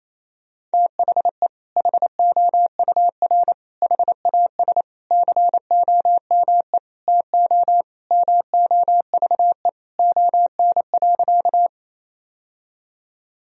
Morse code, heard as a keyed tone, about 28 wpm, 700 Hz. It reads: THE HOUR HAS COME TO MOVE ON.
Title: Sentences from Top 1000 Words - Review Part 1 - 28wpm